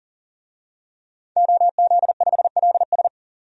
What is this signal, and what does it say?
Morse code, heard as a keyed tone, about 39 wpm, 700 Hz. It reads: OZ5LS
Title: DXCC-100 Call Signs - 5 Characters - 40wpm